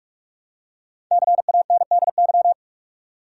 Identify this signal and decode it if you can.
Morse code, heard as a keyed tone, about 41 wpm, 700 Hz. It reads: CANDY
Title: Five Letter Words - 45wpm